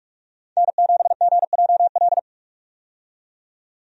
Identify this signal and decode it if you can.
Morse code, heard as a keyed tone, about 41 wpm, 700 Hz. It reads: N7GJL
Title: USA Call Signs - 5 Characters - 45wpm